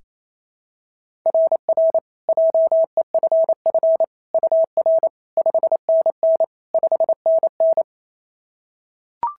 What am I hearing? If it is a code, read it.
Morse code, heard as a keyed tone, about 25 wpm, 650 Hz. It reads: RR JEFF UR 5NN 5NN
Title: SOTA QSO Practice 28wpm